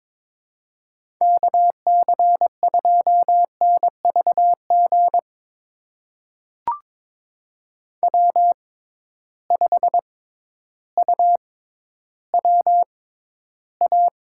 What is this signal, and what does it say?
Morse code, heard as a keyed tone, about 22 wpm, spaced as for 12 wpm, 700 Hz. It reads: KC2NVG W5UWA
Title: USA Call Signs - 6x Character Spacing - 22wpm